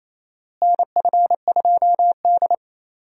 Morse code, heard as a keyed tone, about 28 wpm, 700 Hz. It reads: NF2D
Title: USA Call Signs - 4 Characters - Repeated 3x - 28wpm